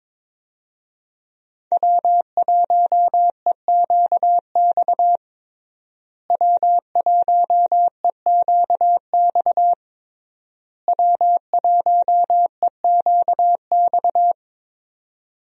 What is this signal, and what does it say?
Morse code, heard as a keyed tone, about 21 wpm, 700 Hz. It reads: W1EQX W1EQX W1EQX
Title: USA Call Signs - 5 Characters - Repeated 3x - 22wpm